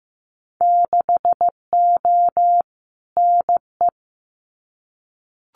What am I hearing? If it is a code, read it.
Morse code, heard as a keyed tone, about 15 wpm, 700 Hz. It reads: 6O NE
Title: Winter Field Day Exchanges Encore 15wpm